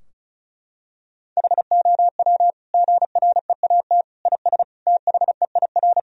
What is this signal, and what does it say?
Morse code, heard as a keyed tone, about 36 wpm, 700 Hz. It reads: HOW GREAT IS THEIR
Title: Sentences from Top 1000 Words - Review Part 4 - 35wpm